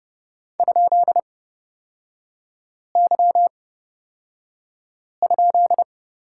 Morse code, heard as a keyed tone, about 29 wpm, spaced as for 5 wpm, 700 Hz. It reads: ?Y?
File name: Lesson 36e - BT - Three Characters 30wpm